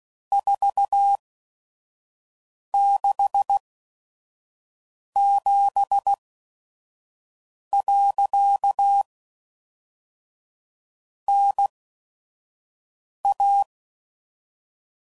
Morse code, heard as a keyed tone, about 16 wpm, spaced as for 5 wpm, 800 Hz. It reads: 467. NA
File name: CW Code Groups @5WPM / QSOs / kc7sac